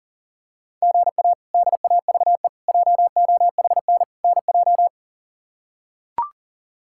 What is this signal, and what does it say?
Morse code, heard as a keyed tone, about 40 wpm, 700 Hz. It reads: GA DAVE JOHN NJ
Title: K1USN SST QSO Practice - 40wpm